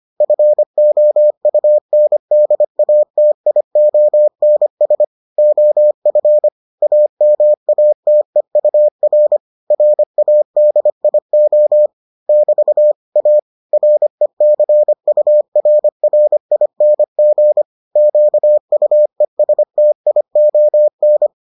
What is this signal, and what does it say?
Morse code, heard as a keyed tone, about 25 wpm, 600 Hz. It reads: FOUNDATIONS OF AMATEUR RADIO <BT> A RECURRING QUESTION